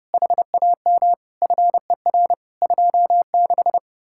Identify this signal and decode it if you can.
Morse code, heard as a keyed tone, about 36 wpm, 700 Hz. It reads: HAM FER 26
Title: Speed-Racing Common QSO Phrases - 30wpm